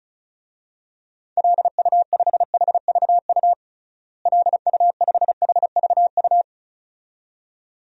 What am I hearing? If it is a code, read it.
Morse code, heard as a keyed tone, about 36 wpm, 700 Hz. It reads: LU5HVU LU5HVU
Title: DXCC-100 Call Signs - 6 Characters - Repeated 2x - 35wpm